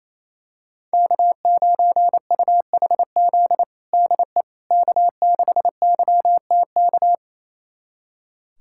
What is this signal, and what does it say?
Morse code, heard as a keyed tone, about 28 wpm, 700 Hz. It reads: K9UHZ DE K6YTK
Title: USA Callsign DE Callsign - 5 Characters - 28wpm